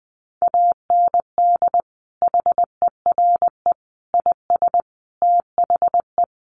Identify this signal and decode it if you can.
Morse code, heard as a keyed tone, about 20 wpm, 700 Hz. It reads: AND HERE IS THE
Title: Sentences from Top 1100 Words 20wpm